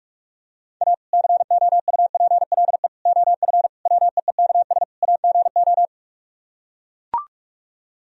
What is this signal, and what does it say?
Morse code, heard as a keyed tone, about 44 wpm, 700 Hz. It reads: A COUPLE OF WEEKS AGO